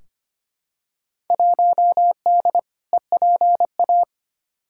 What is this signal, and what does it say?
Morse code, heard as a keyed tone, about 25 wpm, 700 Hz. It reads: 1D EPA
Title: Field Day Exchanges - 25wpm